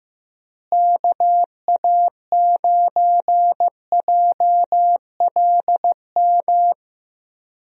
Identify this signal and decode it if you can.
Morse code, heard as a keyed tone, about 15 wpm, 700 Hz. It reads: KA9JLM
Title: USA Call Signs - 6 Characters - Repeated 2x - 15wpm